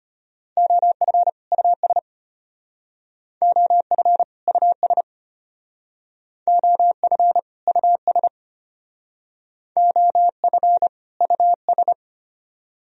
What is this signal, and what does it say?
Morse code, heard as a keyed tone, about 31 wpm, 700 Hz. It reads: OF US OF US OF US OF US